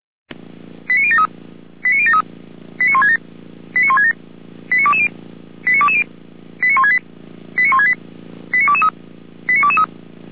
Five-tone selective call (ZVEI), heard as quick runs of five tones, tones of about 80 ms. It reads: 89073 89073 89167 89167 89220 89220 89168 89168 89223 89223
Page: SDR/ZVEI